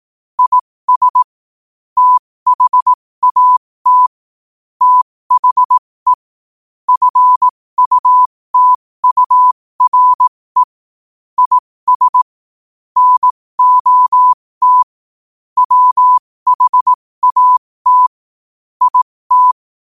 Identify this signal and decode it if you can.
Morse code, heard as a keyed tone, about 18 wpm, spaced as for 15 wpm, 1000 Hz. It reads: IS THAT THE FUTURE IS NOT WHAT IT
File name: qotdfiles / 15wpm